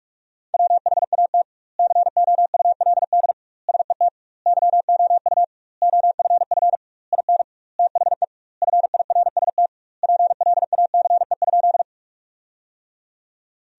Morse code, heard as a keyed tone, about 46 wpm, 700 Hz. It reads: WHAT COULD SET YOU OFF IN THE FIRST PLACE?
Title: 2x Word-Spacing - Sentences from Top 200 Words - 45wpm